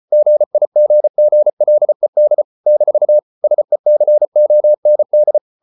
Morse code, heard as a keyed tone, about 34 wpm, 600 Hz. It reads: GIGGLED - SECOND